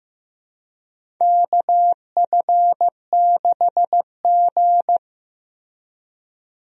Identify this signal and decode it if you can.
Morse code, heard as a keyed tone, about 15 wpm, 700 Hz. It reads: KF6G
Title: USA Call Signs - 4 Characters - Repeated 3x - 15wpm